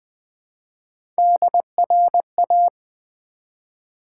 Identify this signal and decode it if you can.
Morse code, heard as a keyed tone, about 20 wpm, 700 Hz. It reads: DRA
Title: Build A Word - 1 Letter At A Time - 5-Letter Words - 20wpm